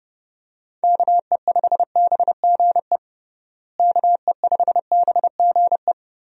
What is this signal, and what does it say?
Morse code, heard as a keyed tone, about 32 wpm, 700 Hz. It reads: KE5BGE KE5BGE
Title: USA Call Signs - 6 Characters - Repeated 2x - 30wpm